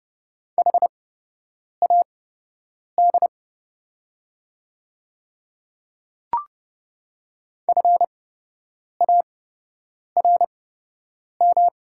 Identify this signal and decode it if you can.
Morse code, heard as a keyed tone, about 30 wpm, spaced as for 8 wpm, 700 Hz. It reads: HAD FARM
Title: 8x Character Spacing - Top 500 Words - 30wpm